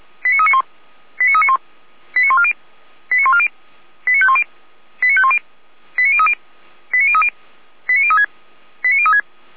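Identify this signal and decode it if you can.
Five-tone selective call (ZVEI), heard as quick runs of five tones, tones of about 80 ms. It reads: 89391 89391 89140 89140 89510 89510 89030 89030 89036 89036
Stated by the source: ZVEI five-tone selective call received with an SDR.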